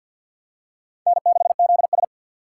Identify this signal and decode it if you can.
Morse code, heard as a keyed tone, about 53 wpm, 700 Hz. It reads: N6ZS
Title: USA Call Signs - 4 Characters - 6x Character Spacing - 50wpm